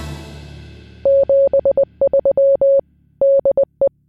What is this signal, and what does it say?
Morse code, heard as a keyed tone, about 20 wpm, 550 Hz. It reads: 73 DE